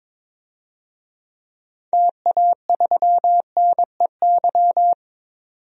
Morse code, heard as a keyed tone, about 22 wpm, 700 Hz. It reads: TA3NEY
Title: DXCC-100 Call Signs - 6 Characters - 22wpm